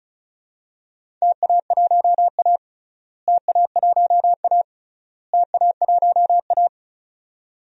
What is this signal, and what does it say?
Morse code, heard as a keyed tone, about 34 wpm, 700 Hz. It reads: TA1A TA1A TA1A
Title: DXCC-100 Call Signs - 4 Characters - Repeated 3x - 35wpm